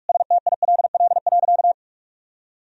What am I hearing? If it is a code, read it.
Morse code, heard as a keyed tone, about 46 wpm, 700 Hz. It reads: STILL.
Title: Sentences from Top 1000 Words - Review Part 1 - 45wpm